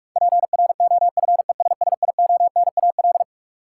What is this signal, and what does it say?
Morse code, heard as a keyed tone, about 42 wpm, 700 Hz. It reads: PROFESSIONAL
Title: Sentences from Top 1400 Words - New Word Drill Alt - 45wpm